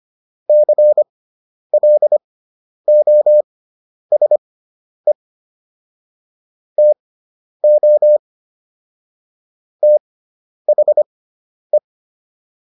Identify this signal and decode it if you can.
Morse code, heard as a keyed tone, about 25 wpm, spaced as for 10 wpm, 600 Hz. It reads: CLOSE TO THE